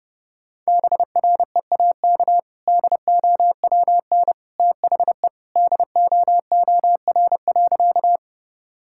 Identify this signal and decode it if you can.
Morse code, heard as a keyed tone, about 30 wpm, 700 Hz. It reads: BREAK DOWN THE DOOR.
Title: Sentences from Top 1000 Words - Review Part 6 - 30wpm